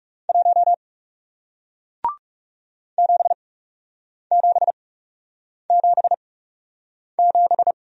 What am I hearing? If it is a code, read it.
Morse code, heard as a keyed tone, about 40 wpm, spaced as for 8 wpm, 700 Hz. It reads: 1 7777